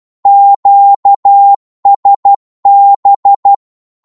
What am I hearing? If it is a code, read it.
Morse code, heard as a keyed tone, about 12 wpm, 800 Hz. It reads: QSB
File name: Morse / qso-12wpm